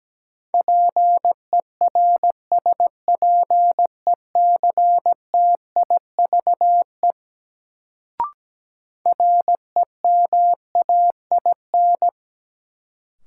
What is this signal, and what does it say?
Morse code, heard as a keyed tone, about 17 wpm, 700 Hz. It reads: PERSPECTIVE REMAIN
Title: Sentences from Top 1200 Words - New Word Drill - 17wpm